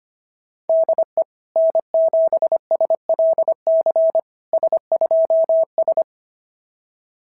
Morse code, heard as a keyed tone, about 25 wpm, 660 Hz. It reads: DE N7SLC S2S
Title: SOTA QSO Practice 25wpm